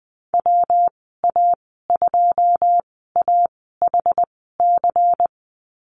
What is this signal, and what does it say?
Morse code, heard as a keyed tone, about 20 wpm, spaced as for 15 wpm, 700 Hz. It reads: WA2AHC
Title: 122 Lesson 20g - Number 3 - Call Signs 20wpm